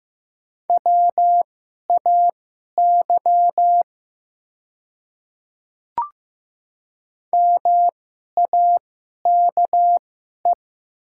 Morse code, heard as a keyed tone, about 15 wpm, spaced as for 11 wpm, 700 Hz. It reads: WAY MAKE
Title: 2x Character Spacing - Top 200 Words - 15wpm